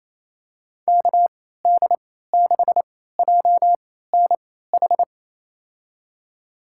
Morse code, heard as a keyed tone, about 28 wpm, spaced as for 16 wpm, 700 Hz. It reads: KD6JNH
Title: USA Call Signs - 6 Characters - 3x Character Spacing - 28wpm